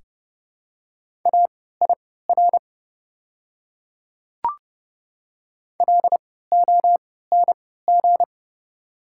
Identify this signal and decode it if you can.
Morse code, heard as a keyed tone, about 30 wpm, spaced as for 17 wpm, 700 Hz. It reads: AIR LONG
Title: 3x Character Spacing - Top 500 Words - 30wpm